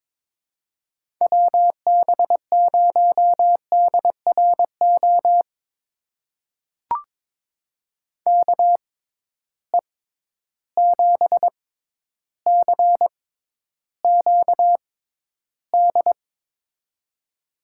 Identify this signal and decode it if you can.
Morse code, heard as a keyed tone, about 21 wpm, spaced as for 11 wpm, 700 Hz. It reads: WB0DRO KE7CQD
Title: USA Call Signs - 6 Characters - 6x Character Spacing - 22wpm